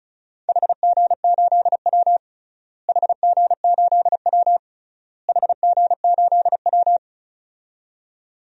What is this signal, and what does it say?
Morse code, heard as a keyed tone, about 36 wpm, 700 Hz. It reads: HG8W HG8W HG8W
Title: DXCC-100 Call Signs - 4 Characters - Repeated 3x - 35wpm